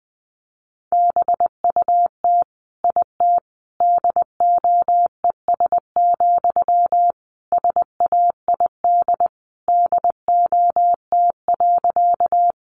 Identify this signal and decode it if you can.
Morse code, heard as a keyed tone, about 19 wpm, 700 Hz. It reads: BUT IT DOES, SAID DOT.
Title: Sentences from Top 1500 Words 20wpm